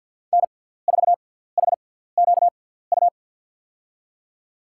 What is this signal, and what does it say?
Morse code, heard as a keyed tone, about 53 wpm, spaced as for 17 wpm, 700 Hz. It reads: N4HQU
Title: USA Call Signs - 5 Characters - 6x Character Spacing - 50wpm